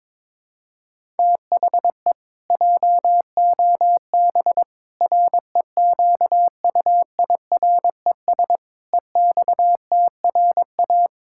Morse code, heard as a keyed tone, about 22 wpm, 700 Hz. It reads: THE JOB REQUIRES EXTRA
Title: Pangram Challenge v2 - 22wpm